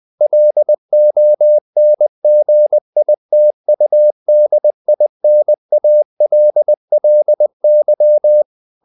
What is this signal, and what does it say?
Morse code, heard as a keyed tone, about 20 wpm, 600 Hz. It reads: LONGITUDINALLY
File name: the old man and the sea / the old man and the sea part 1 20wpm